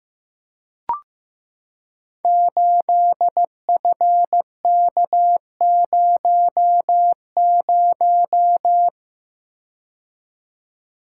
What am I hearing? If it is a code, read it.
Morse code, heard as a keyed tone, about 15 wpm, 700 Hz. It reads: 8FK00
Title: Code Groups - 5 Letter:Numbers - 15wpm